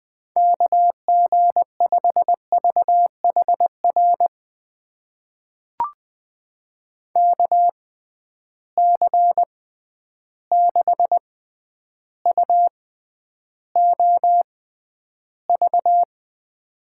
Morse code, heard as a keyed tone, about 20 wpm, spaced as for 10 wpm, 700 Hz. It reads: KG5VHR KC6UOV
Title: USA Call Signs - 6 Characters - 6x Character Spacing - 20wpm